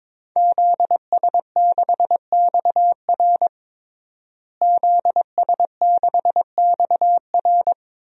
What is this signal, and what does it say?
Morse code, heard as a keyed tone, about 21 wpm, 700 Hz. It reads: ZS6XR ZS6XR
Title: DXCC-100 Call Signs - 5 Characters - Repeated 3x - 22wpm